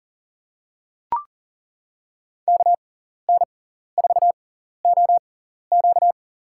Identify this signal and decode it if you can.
Morse code, heard as a keyed tone, about 40 wpm, spaced as for 14 wpm, 700 Hz. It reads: KN4OQ
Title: USA Call Signs - 5 Characters - 6x Character Spacing - 40wpm